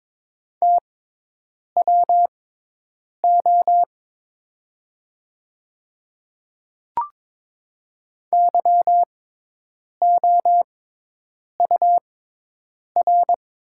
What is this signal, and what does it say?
Morse code, heard as a keyed tone, about 21 wpm, spaced as for 8 wpm, 700 Hz. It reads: TWO YOUR
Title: 6x Character Spacing - Top 100 Words - 22wpm